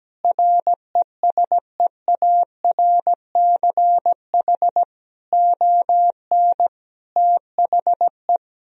Morse code, heard as a keyed tone, about 17 wpm, 700 Hz. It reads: RESEARCH ON THE
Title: Sentences from Top 1000 Words - Review Part 4 - 17wpm